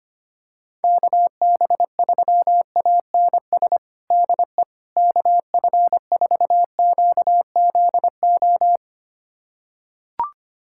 Morse code, heard as a keyed tone, about 25 wpm, 700 Hz. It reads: KB3ANS DE KF4QZO
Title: USA Callsign DE Callsign - 6 Characters - 25wpm